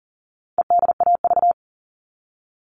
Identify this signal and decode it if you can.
Morse code, heard as a keyed tone, about 40 wpm, 700 Hz. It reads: EDAV
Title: All Cut Numbers - 4-Digit 40wpm